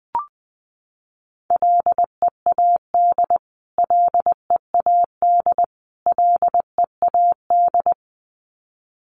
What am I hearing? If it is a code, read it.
Morse code, heard as a keyed tone, about 19 wpm, 700 Hz. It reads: LEAD LEAD LEAD
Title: Top 500 Words Repeated 20wpm